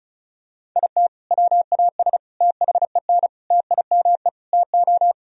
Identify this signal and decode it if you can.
Morse code, heard as a keyed tone, about 36 wpm, 700 Hz. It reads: IT WAS THEN TIME TO
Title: Sentences from Top 1000 Words - Review Part 1 - 35wpm